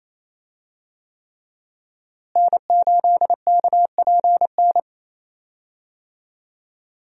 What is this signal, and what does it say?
Morse code, heard as a keyed tone, about 28 wpm, 700 Hz. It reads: N8KPN
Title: USA Call Signs - 5 Characters - Repeated 3x - 28wpm